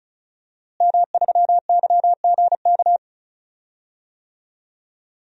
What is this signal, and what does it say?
Morse code, heard as a keyed tone, about 36 wpm, 700 Hz. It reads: M3YGK
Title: DXCC-100 Call Signs - 5 Characters - Repeated 2x - 35wpm